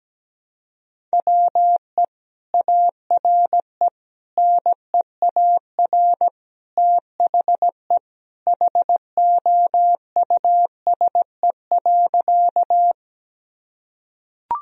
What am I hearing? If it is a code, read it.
Morse code, heard as a keyed tone, about 17 wpm, 700 Hz. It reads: WE ARE NEAR THE HOUSE.
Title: Sentences from Top 1000 Words - Review Part 1 - 17wpm